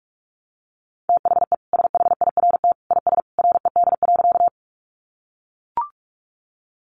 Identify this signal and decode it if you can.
Morse code, heard as a keyed tone, about 44 wpm, 700 Hz. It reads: THE SHIRT IS RED.
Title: Additional Easy Phrases 45wpm